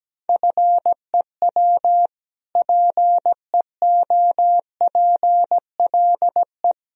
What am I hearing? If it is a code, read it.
Morse code, heard as a keyed tone, about 17 wpm, 700 Hz. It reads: FEW PEOPLE
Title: Sentences from Top 1000 Words - Review Part 1 - 17wpm